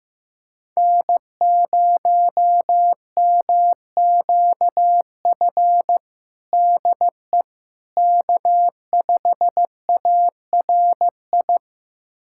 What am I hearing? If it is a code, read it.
Morse code, heard as a keyed tone, about 15 wpm, 700 Hz. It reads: N0MQF DE K5ARI
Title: USA Callsign DE Callsign 15wpm